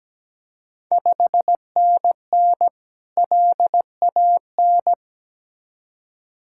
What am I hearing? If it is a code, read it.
Morse code, heard as a keyed tone, about 17 wpm, 700 Hz. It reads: 5NN LAN
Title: Ontario QSO Party Exchange 17wpm